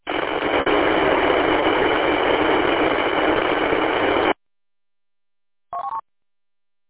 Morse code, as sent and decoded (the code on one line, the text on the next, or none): none